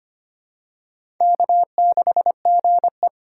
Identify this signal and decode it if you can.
Morse code, heard as a keyed tone, about 25 wpm, 690 Hz. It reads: K6GE
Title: USA Call Signs - 4 Characters - 4x Character Spacing - 25wpm